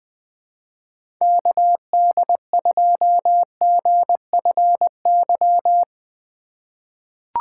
Morse code, heard as a keyed tone, about 20 wpm, 700 Hz. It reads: KD2GFY
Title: USA Call Signs - 6 Characters - 4x Character Spacing - 20wpm